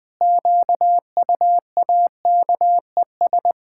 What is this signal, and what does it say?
Morse code, heard as a keyed tone, about 20 wpm, 700 Hz. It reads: QUAKES
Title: Pangram Challenge v2 - 20wpm